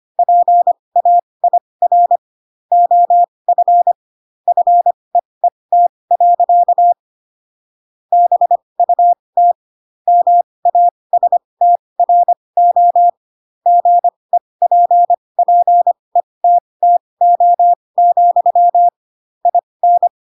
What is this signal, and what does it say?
Morse code, heard as a keyed tone, about 25 wpm, spaced as for 20 wpm, 700 Hz. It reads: PAIR OF FEET. BUT MASTRO GEPPETTO, IN
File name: Pinocchio / Pinocchio ch8 15 25 ik1hge